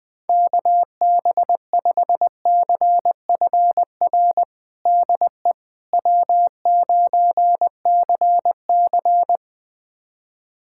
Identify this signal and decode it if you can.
Morse code, heard as a keyed tone, about 20 wpm, 700 Hz. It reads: KB5CFR DE W9CC
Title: USA Callsign DE Callsign 20wpm